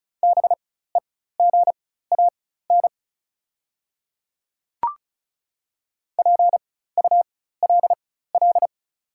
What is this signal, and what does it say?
Morse code, heard as a keyed tone, about 34 wpm, spaced as for 16 wpm, 690 Hz. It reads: BEGAN PULL
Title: 4x Character Spacing - Top 500 Words - 35wpm